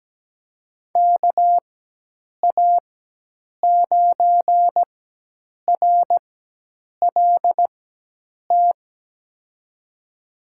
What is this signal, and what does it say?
Morse code, heard as a keyed tone, about 17 wpm, spaced as for 8 wpm, 700 Hz. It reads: KA9RLT
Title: USA Call Signs - 6 Characters - 4x Character Spacing - 17wpm